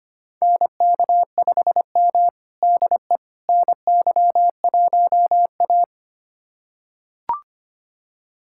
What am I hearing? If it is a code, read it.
Morse code, heard as a keyed tone, about 25 wpm, 700 Hz. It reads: NK5M DE NY1A
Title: USA Callsign DE Callsign - 4 Characters - 25wpm